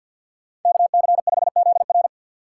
Morse code, heard as a keyed tone, about 47 wpm, 700 Hz. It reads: KC5ZR
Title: USA Call Signs - 5 Characters - 2x Character Spacing - 50wpm